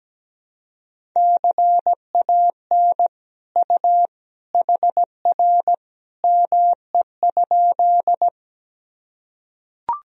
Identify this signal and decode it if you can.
Morse code, heard as a keyed tone, about 17 wpm, 700 Hz. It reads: CAN U HR ME?
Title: Sentences from Top 300 Words and Abbreviations 17wpm